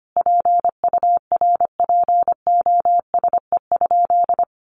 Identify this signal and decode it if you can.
Morse code, heard as a keyed tone, about 25 wpm, 700 Hz. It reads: PURPOSE?
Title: Sentences from Top 1500 Words 25wpm